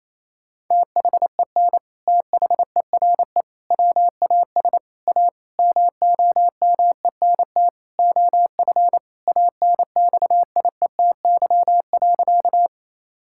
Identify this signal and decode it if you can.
Morse code, heard as a keyed tone, about 28 wpm, 700 Hz. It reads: THEN THERE WAS A MOMENT OF ANXIETY.
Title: Sentences from Top 1600 Words 28wpm